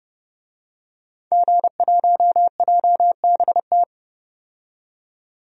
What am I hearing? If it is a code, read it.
Morse code, heard as a keyed tone, about 30 wpm, 700 Hz. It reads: G1JBT
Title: DXCC-100 Call Signs - 5 Characters - 30wpm